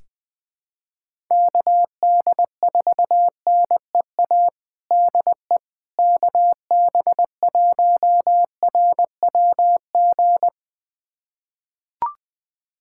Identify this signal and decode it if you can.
Morse code, heard as a keyed tone, about 20 wpm, 700 Hz. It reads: KD4NEA DE KB1RWG
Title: USA Callsign DE Callsign - 6 Characters - 20wpm